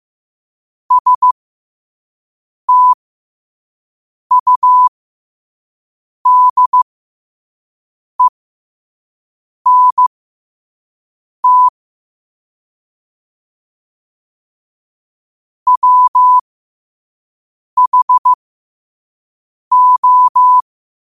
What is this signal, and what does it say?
Morse code, heard as a keyed tone, about 15 wpm, spaced as for 5 wpm, 1000 Hz. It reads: STUDENT WHO
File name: qotdfiles / 5wpm